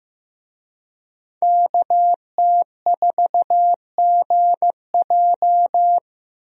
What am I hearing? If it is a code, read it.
Morse code, heard as a keyed tone, about 15 wpm, 700 Hz. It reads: KT4GJ
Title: USA Call Signs - 5 Characters - Repeated 2x - 15wpm